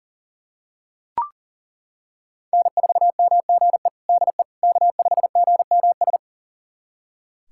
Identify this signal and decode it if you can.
Morse code, heard as a keyed tone, about 40 wpm, 700 Hz. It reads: N4MGE DE K5GMS
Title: USA Callsign DE Callsign - 5 Characters - 40wpm